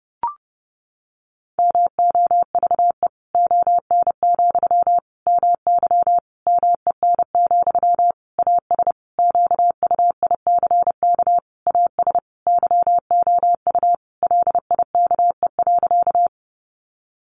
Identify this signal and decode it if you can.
Morse code, heard as a keyed tone, about 28 wpm, 700 Hz. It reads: MOVE ON, MY MEN, AS QUICK AS YOU LIKE.
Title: Sentences from Top 500 Words 30wpm